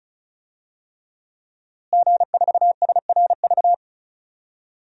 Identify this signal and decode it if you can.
Morse code, heard as a keyed tone, about 36 wpm, 700 Hz. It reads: G4SRV
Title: DXCC-100 Call Signs - 5 Characters - Repeated 3x - 35wpm